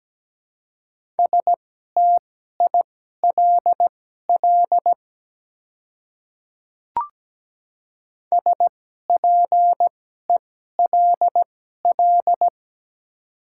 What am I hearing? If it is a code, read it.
Morse code, heard as a keyed tone, about 17 wpm, spaced as for 12 wpm, 700 Hz. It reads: STILL SPELL
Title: 2x Character Spacing - Top 300 Words - 17wpm